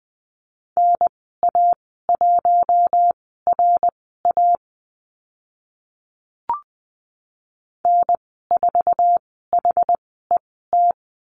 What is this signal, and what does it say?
Morse code, heard as a keyed tone, about 20 wpm, spaced as for 14 wpm, 700 Hz. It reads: NA1RA N4HET
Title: 076 Lesson 13g - Letter L - Call Signs 20wpm